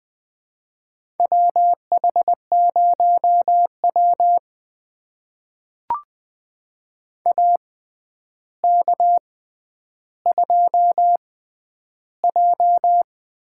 Morse code, heard as a keyed tone, about 21 wpm, spaced as for 10 wpm, 700 Hz. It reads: WH0W AK2J
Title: USA Call Signs - 4 Characters - 6x Character Spacing - 20wpm